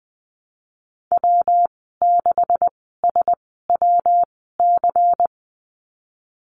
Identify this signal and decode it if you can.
Morse code, heard as a keyed tone, about 20 wpm, spaced as for 15 wpm, 700 Hz. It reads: W6SWC
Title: 173 Lesson 28g - Number 9 - Call Signs 20wpm